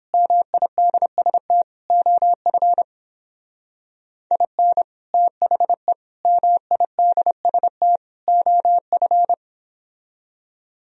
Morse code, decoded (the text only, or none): MIDST OF IN THE MIDST OF